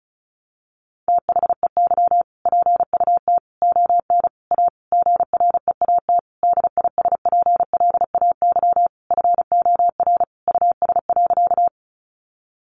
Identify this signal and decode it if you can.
Morse code, heard as a keyed tone, about 34 wpm, 700 Hz. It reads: THEY PUT ON A GREAT DISPLAY FOR US.
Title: Sentences from Top 1200 Words 35wpm